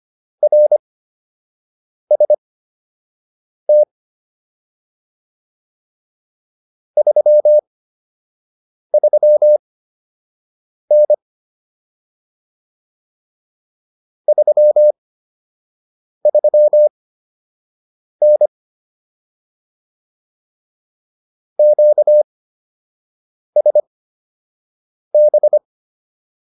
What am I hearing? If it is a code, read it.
Morse code, heard as a keyed tone, about 25 wpm, spaced as for 6 wpm, 600 Hz. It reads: RST 33N 33N QSB
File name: QSO 105 06